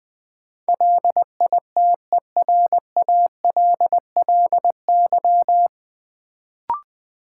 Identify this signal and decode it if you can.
Morse code, heard as a keyed tone, about 20 wpm, 700 Hz. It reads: LITERALLY
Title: Sentences from Top 1400 Words - New Word Drill - 20wpm